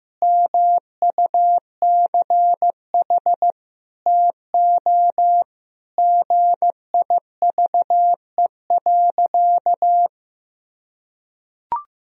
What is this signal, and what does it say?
Morse code, heard as a keyed tone, about 16 wpm, 700 Hz. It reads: MUCH TO GIVE.
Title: Sentences from Top 1000 Words - Review Part 4 - 15wpm